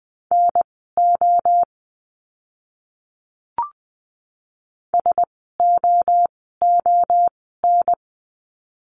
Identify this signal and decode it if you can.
Morse code, heard as a keyed tone, about 20 wpm, spaced as for 14 wpm, 700 Hz. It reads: NO SOON
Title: 036 Lesson 7f - Letter s - Words 20wpm